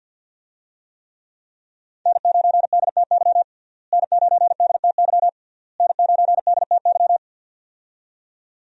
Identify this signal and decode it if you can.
Morse code, heard as a keyed tone, about 47 wpm, 700 Hz. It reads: N9DTY N9DTY N9DTY
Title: USA Call Signs - 5 Characters - Repeated 3x - 50wpm